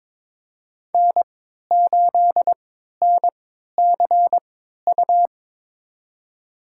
Morse code, heard as a keyed tone, about 22 wpm, spaced as for 12 wpm, 700 Hz. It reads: N8NCU
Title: USA Call Signs - 3x Character Spacing - 22wpm